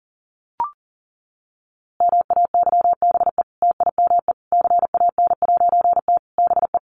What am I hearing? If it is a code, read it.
Morse code, heard as a keyed tone, about 40 wpm, 700 Hz. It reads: MAYBE TIME CAN'T BE
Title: A Sound Of Thunder 40wpm